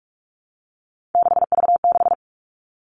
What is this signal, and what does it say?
Morse code, heard as a keyed tone, about 46 wpm, 700 Hz. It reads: 6V6
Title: Three Challenging Letter-Number Combinations 45wpm